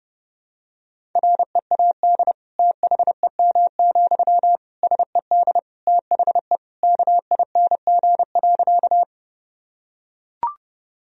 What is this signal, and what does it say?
Morse code, heard as a keyed tone, about 31 wpm, 700 Hz. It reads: READ THEM, SED THE KING.
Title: Sentences from Top 400 Words and Abbreviations 30wpm